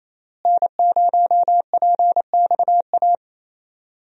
Morse code, heard as a keyed tone, about 27 wpm, 700 Hz. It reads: N0PXA
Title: USA Call Signs - 5 Characters - Repeated 3x - 28wpm